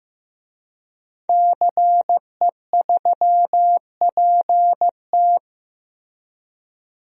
Morse code, heard as a keyed tone, about 15 wpm, 700 Hz. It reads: CE3PT
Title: DXCC-100 Call Signs - 5 Characters - Repeated 2x - 15wpm